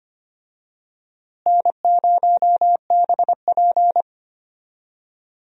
Morse code, heard as a keyed tone, about 25 wpm, 700 Hz. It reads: N0BP
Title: USA Call Signs - 4 Characters - Repeated 3x - 25wpm